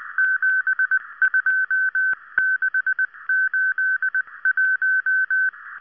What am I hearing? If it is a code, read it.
Morse code, heard as a keyed tone, about 19 wpm, 1550 Hz. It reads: 72681